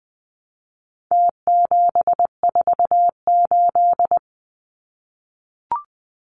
Morse code, heard as a keyed tone, about 20 wpm, 700 Hz. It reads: T748
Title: Numbers - Contest Style - 4-Digit 20wpm